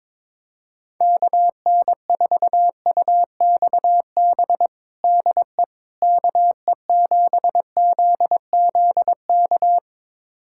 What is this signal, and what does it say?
Morse code, heard as a keyed tone, about 22 wpm, 700 Hz. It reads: KN4UXB DE KE7ZZK
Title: USA Callsign DE Callsign - 6 Characters - 22wpm